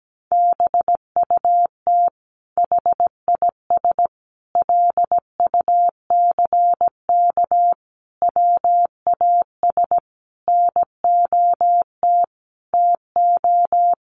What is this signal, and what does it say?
Morse code, heard as a keyed tone, about 17 wpm, 700 Hz. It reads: BUT HIS LUCK WAS NOT TO
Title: Sentences from Top 1500 Words 17wpm